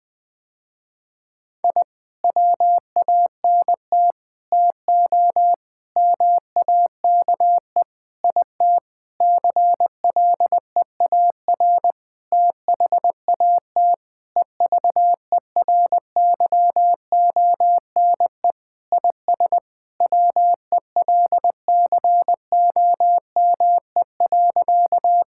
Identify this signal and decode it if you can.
Morse code, heard as a keyed tone, about 21 wpm, 700 Hz. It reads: I WANT TO MAKE IT CLEAR THAT EVERYONE IS WELCOME.
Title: Sentences with 7-Word Phrases - Spoken Beginning - 20wpm